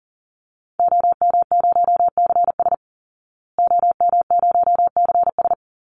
Morse code, heard as a keyed tone, about 40 wpm, 700 Hz. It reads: OM0CS OM0CS
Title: Call Signs Repeated 2x - 40wpm